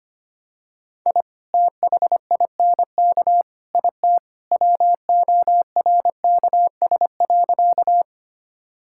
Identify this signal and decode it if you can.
Morse code, heard as a keyed tone, about 25 wpm, 700 Hz. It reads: I THINK IT WORKS.
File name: practice / 25wpm / Sentences with 2-Word Phrases - 25wpm